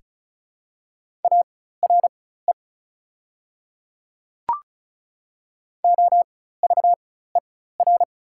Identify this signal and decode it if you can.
Morse code, heard as a keyed tone, about 36 wpm, spaced as for 16 wpm, 700 Hz. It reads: ARE OVER
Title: 4x Character Spacing - Top 100 Words - 35wpm